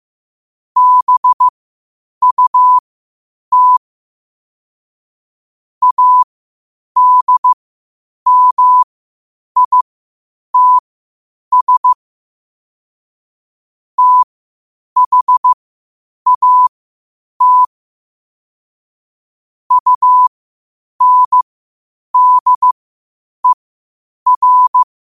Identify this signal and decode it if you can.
Morse code, heard as a keyed tone, about 15 wpm, spaced as for 8 wpm, 1000 Hz. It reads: BUT ADMITS THAT UNDER